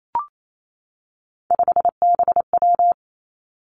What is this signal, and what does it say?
Morse code, heard as a keyed tone, about 28 wpm, 700 Hz. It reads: 5BW
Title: Three Challenging Letter-Number Combinations 28wpm